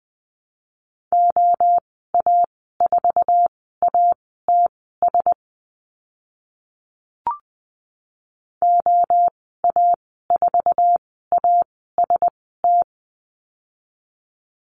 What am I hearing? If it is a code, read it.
Morse code, heard as a keyed tone, about 20 wpm, spaced as for 14 wpm, 700 Hz. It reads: OA4ATS OA4AST
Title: 069 Lesson 12g - Letter d - Call Signs 20wpm